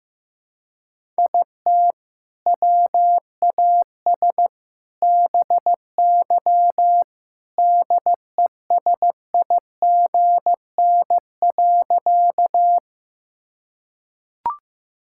Morse code, heard as a keyed tone, about 15 wpm, 700 Hz. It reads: IT WAS BY DESIGN.
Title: Sentences from Top 1000 Words - Review Part 5 - 15wpm